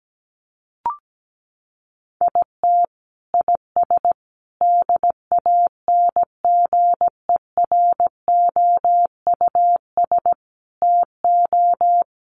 Morse code, heard as a keyed tone, about 17 wpm, 700 Hz. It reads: IT IS DANGEROUS TO
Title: Sentences from Top 1200 Words 17wpm